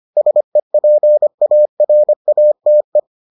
Morse code, heard as a keyed tone, about 25 wpm, 600 Hz. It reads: SEPARATE